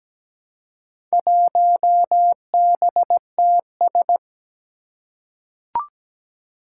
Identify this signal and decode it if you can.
Morse code, heard as a keyed tone, about 17 wpm, 700 Hz. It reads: 1BTS
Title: Four Challenging Letter-Number Combinations 17wpm